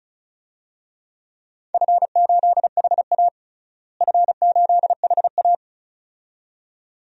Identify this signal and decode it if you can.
Morse code, heard as a keyed tone, about 36 wpm, 700 Hz. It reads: F8HA F8HA
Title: DXCC-100 Call Signs - 4 Characters - Repeated 2x - 35wpm